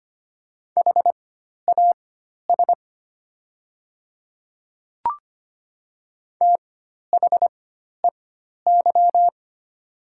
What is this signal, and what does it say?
Morse code, heard as a keyed tone, about 25 wpm, spaced as for 12 wpm, 700 Hz. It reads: HAS THEY
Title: 4x Character Spacing - Top 100 Words - 25wpm